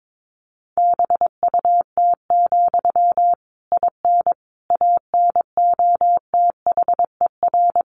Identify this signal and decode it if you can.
Morse code, heard as a keyed tone, about 22 wpm, 700 Hz. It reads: BUT, IN ANOTHER
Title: Sentences from Top 1500 Words 22wpm